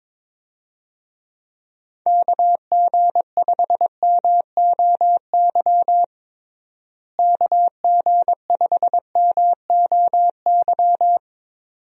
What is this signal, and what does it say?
Morse code, heard as a keyed tone, about 22 wpm, 700 Hz. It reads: KG5MOY KG5MOY
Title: USA Call Signs - 6 Characters - Repeated 2x - 22wpm